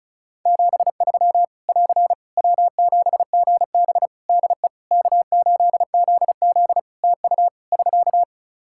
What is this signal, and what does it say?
Morse code, heard as a keyed tone, about 36 wpm, 700 Hz. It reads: 73 <AR> W7GB DE K8ZZ TU <SK>
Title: Practice QSO - Exchange 4 - Ending - 35wpm